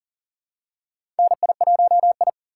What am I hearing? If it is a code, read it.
Morse code, heard as a keyed tone, about 40 wpm, 700 Hz. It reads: NI1I
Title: USA Call Signs - 4 Characters - 2x Character Spacing - 40wpm